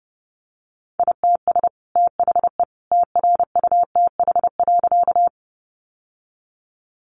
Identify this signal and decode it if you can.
Morse code, heard as a keyed tone, about 30 wpm, 700 Hz. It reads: ITS THE TRUTH.
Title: Sentences from Top 800 Words 30wpm